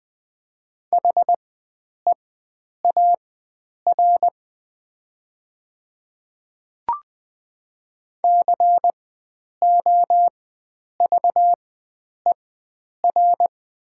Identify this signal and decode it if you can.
Morse code, heard as a keyed tone, about 20 wpm, spaced as for 9 wpm, 700 Hz. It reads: HEAR COVER
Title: 4x Character Spacing - Top 500 Words - 20wpm